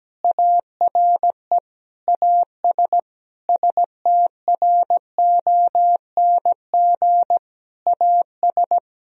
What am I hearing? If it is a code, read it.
Morse code, heard as a keyed tone, about 17 wpm, 700 Hz. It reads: ARE AS STRONG AS
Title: Sentences from Top 1000 Words - Review Part 4 - 17wpm